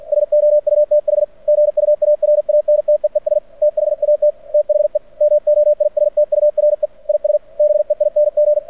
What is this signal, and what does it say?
Morse code, heard as a keyed tone, about 49 wpm, 600 Hz. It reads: SOUTH GUARANTEES THAT THE MOISTURE IS BEING
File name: archived / CRSnet